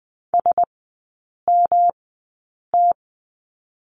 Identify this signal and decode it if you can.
Morse code, heard as a keyed tone, about 20 wpm, spaced as for 8 wpm, 690 Hz. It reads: SMT
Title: 198 Lesson 32e - Letter k - Three Characters 20wpm